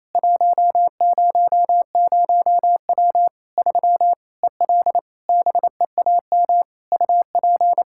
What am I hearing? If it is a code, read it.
Morse code, heard as a keyed tone, about 28 wpm, 700 Hz. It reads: 100W 3 EL BEAM UP
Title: Practice QSO - Exchange 2 - Info - 28wpm